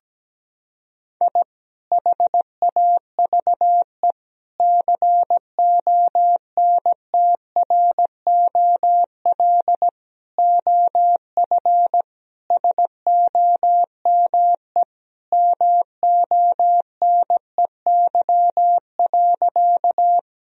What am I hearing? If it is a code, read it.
Morse code, heard as a keyed tone, about 17 wpm, 700 Hz. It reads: I HAVE CONTROL OF SOME MONEY.
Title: Sentences from Top 1000 Words - Review Part 3 - 17wpm